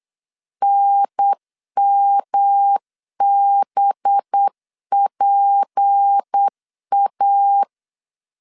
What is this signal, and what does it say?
Morse code, heard as a keyed tone, about 8 wpm, 800 Hz. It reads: NMBPA
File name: LEZIONI / 7 GF9Z / 7 GF9 8wpm 3545